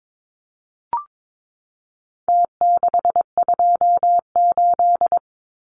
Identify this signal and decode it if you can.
Morse code, heard as a keyed tone, about 22 wpm, 700 Hz. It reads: T628
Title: Numbers - Contest Style - 4-Digit 22wpm